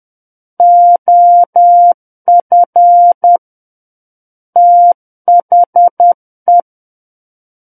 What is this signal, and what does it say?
Morse code, heard as a keyed tone, about 10 wpm, 700 Hz. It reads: OF THE
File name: Book One / 01-Chapter 1 - 10 WPM